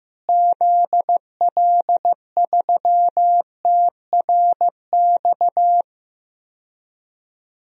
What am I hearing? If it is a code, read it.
Morse code, heard as a keyed tone, about 15 wpm, 700 Hz. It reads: ZL3TRX
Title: DXCC-100 Call Signs - 6 Characters - Repeated 3x - 15wpm